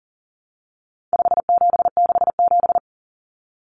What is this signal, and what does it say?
Morse code, heard as a keyed tone, about 40 wpm, 700 Hz. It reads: 5767
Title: Common Cut Numbers - 4-Digit 40wpm